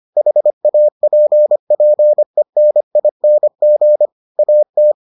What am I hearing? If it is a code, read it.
Morse code, heard as a keyed tone, about 25 wpm, 600 Hz. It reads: HAPPENING AT